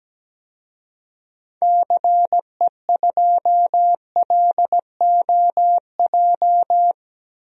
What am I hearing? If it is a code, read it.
Morse code, heard as a keyed tone, about 17 wpm, 700 Hz. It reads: CE2LOJ
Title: DXCC-100 Call Signs - 6 Characters - Repeated 3x - 17wpm